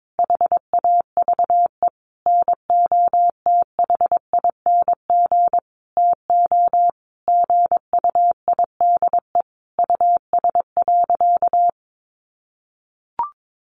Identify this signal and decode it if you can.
Morse code, heard as a keyed tone, about 22 wpm, 700 Hz. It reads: HAVE NOTHING TO GUIDE US.
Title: Sentences from Top 800 Words 22wpm